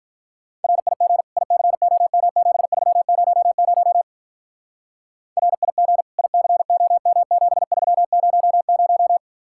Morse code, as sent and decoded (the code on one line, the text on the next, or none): .-. .. --. / .. -.-. --- -- --... ...-- ----- ----- / .-. .. --. / .. -.-. --- -- --... ...-- ----- -----
RIG ICOM7300 RIG ICOM7300